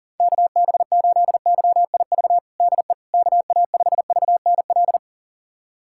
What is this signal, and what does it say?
Morse code, heard as a keyed tone, about 37 wpm, 700 Hz. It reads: KB8YIV DE KA5VNL
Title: USA Callsign DE Callsign - 6 Characters - 40wpm